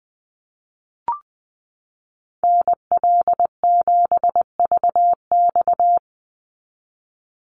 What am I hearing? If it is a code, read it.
Morse code, heard as a keyed tone, about 20 wpm, 700 Hz. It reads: NL7VX
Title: Call Signs - 20wpm